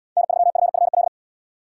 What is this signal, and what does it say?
Morse code, heard as a keyed tone, about 70 wpm, 690 Hz. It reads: N3FFL